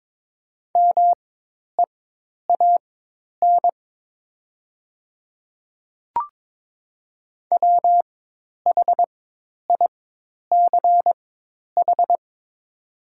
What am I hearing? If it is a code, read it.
Morse code, heard as a keyed tone, about 22 wpm, spaced as for 10 wpm, 700 Hz. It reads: MEAN WHICH
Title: 4x Character Spacing - Top 500 Words - 22wpm